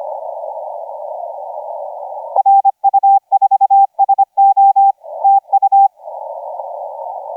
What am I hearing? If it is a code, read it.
Morse code, heard as a keyed tone, about 25 wpm, 790 Hz. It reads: RU4SO TU
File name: SJ0SOP 1650UTC 15m